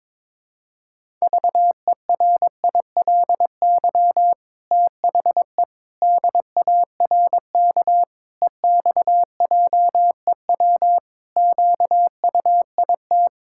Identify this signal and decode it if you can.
Morse code, heard as a keyed tone, about 22 wpm, 700 Hz. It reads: VERILY THE DARK EXJEW QUIT
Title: Pangram Challenge v2 - 22wpm